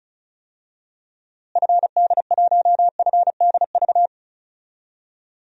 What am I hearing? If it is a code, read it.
Morse code, heard as a keyed tone, about 36 wpm, 700 Hz. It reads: FD1FDV
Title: DXCC-100 Call Signs - 6 Characters - Repeated 3x - 35wpm